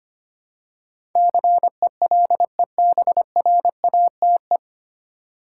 Morse code, heard as a keyed tone, about 25 wpm, 700 Hz. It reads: CELEBRATE
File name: practice / 25wpm / 9 Letter Words - First 3 Letters Given - 25wpm